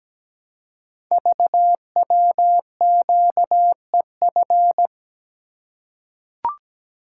Morse code, heard as a keyed tone, about 17 wpm, 700 Hz. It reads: VWQEF
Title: Code Groups - 5 Letters - 17wpm